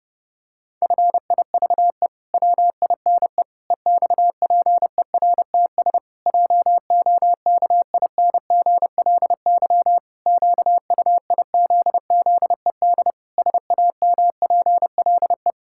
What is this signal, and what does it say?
Morse code, heard as a keyed tone, about 30 wpm, 700 Hz. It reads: FIVE WINE EXPERTS JOKINGLY QUIZZED SAMPLE
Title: Pangram Challenge v2 - 30wpm